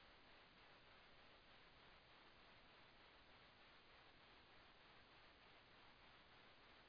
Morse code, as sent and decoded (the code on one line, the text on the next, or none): none